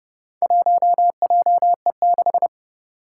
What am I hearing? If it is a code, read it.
Morse code, heard as a keyed tone, about 30 wpm, 700 Hz. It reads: 1JE6
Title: Four Challenging Letter-Number Combinations 30wpm